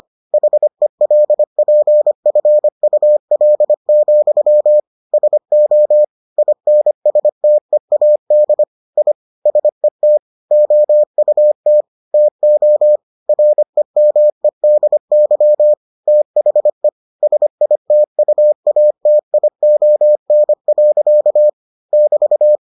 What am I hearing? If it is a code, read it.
Morse code, heard as a keyed tone, about 25 wpm, 600 Hz. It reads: HELPFUL, SO INSTEAD I SET OUT TO REMEDY THE SITUATION. <BT>